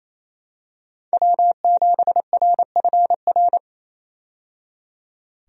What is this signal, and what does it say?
Morse code, heard as a keyed tone, about 28 wpm, 700 Hz. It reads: W7RFR
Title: USA Call Signs - 5 Characters - 28wpm